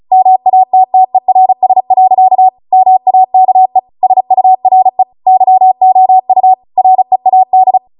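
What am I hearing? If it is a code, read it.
Morse code, heard as a keyed tone, about 36 wpm, 750 Hz. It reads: MATTERS. MAKE SURE YOU READ
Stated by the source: Amateur radio Morse code practice archive transmission at 35 wpm.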